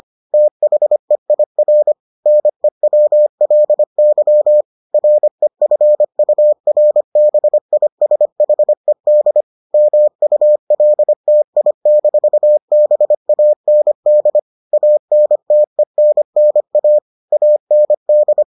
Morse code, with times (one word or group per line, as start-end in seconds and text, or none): 0.3-1.9 THEIR
2.3-4.6 NEWLY
4.9-9.4 REFURBISHED
9.7-14.4 MULTI-BAND
14.7-17.0 ANTENNA
17.3-18.4 AND